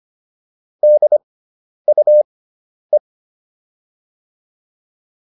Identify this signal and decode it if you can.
Morse code, heard as a keyed tone, about 25 wpm, spaced as for 10 wpm, 600 Hz. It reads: DUE